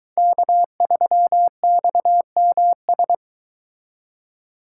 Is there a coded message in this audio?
Yes: Morse code from 0.2 to 3.1 s, heard as a keyed tone, about 23 wpm, 700 Hz. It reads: K3XMS